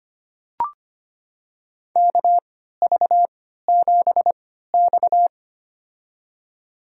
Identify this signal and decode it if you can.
Morse code, heard as a keyed tone, about 25 wpm, spaced as for 14 wpm, 700 Hz. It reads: KV7X
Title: USA Call Signs - 4 Characters - 3x Character Spacing - 25wpm